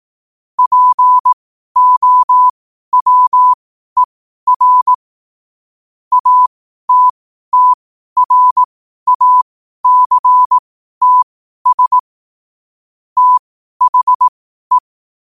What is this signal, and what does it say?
Morse code, heard as a keyed tone, about 18 wpm, spaced as for 12 wpm, 1000 Hz. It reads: POWER ATTRACTS THE
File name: qotdfiles / 12wpm